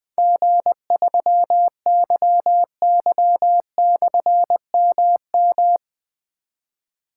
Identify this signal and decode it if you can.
Morse code, heard as a keyed tone, about 20 wpm, 700 Hz. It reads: G3YY/MM
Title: Difficult Call Signs 20wpm